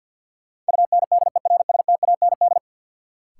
Morse code, heard as a keyed tone, about 51 wpm, 700 Hz. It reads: UNDERSTAND
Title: Sets of 4 Words - New Word Drill - 50wpm